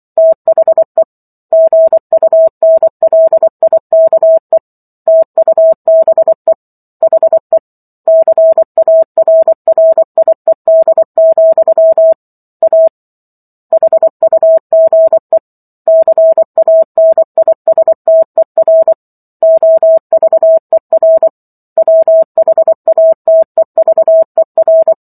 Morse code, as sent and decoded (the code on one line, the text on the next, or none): - .... / --. ..- -. .-.. .. -.- . / - ..- -... / .... / -.-. .- .-. .-. .. -.. --..-- / .- / .... ..- --. / -.-. .- -. .. ... - . .-. / --- ...- . .-. / .-- .... .- - . ...- . .-.
THE GUNLIKE TUBE HE CARRIED, A HUGE CANISTER OVER WHATEVER